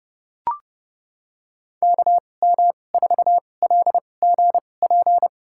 Morse code, heard as a keyed tone, about 30 wpm, spaced as for 22 wpm, 700 Hz. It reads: KM4LGP
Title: USA Call Signs - 6 Characters - 2x Character Spacing - 30wpm